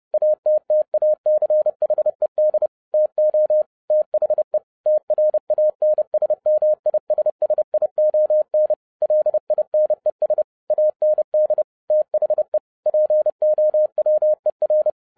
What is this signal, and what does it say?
Morse code, heard as a keyed tone, about 30 wpm, 610 Hz. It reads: ATTACHED TO THE TRANSMISSION LINES AND THE POWER
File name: 2012-01-mp3 / n12-I DIDNT MISS THE SOLAR CYCLE 24-30wpm